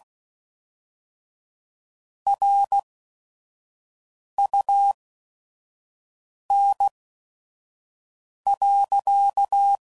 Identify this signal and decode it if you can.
Morse code, heard as a keyed tone, about 16 wpm, spaced as for 5 wpm, 800 Hz. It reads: RUN.
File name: CW Code Groups @5WPM / QSOs / kc7haa